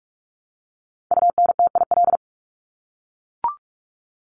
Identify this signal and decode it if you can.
Morse code, heard as a keyed tone, about 44 wpm, 700 Hz. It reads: UNTIL
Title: Top 300 Words Repeated 45wpm